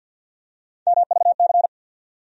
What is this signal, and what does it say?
Morse code, heard as a keyed tone, about 54 wpm, 700 Hz. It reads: MVC
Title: USA Call Sign Postfixes - 3 Letters 50wpm